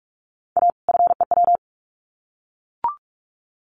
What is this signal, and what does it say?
Morse code, heard as a keyed tone, about 44 wpm, 700 Hz. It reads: A FEW
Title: Sets of 2 Words 45wpm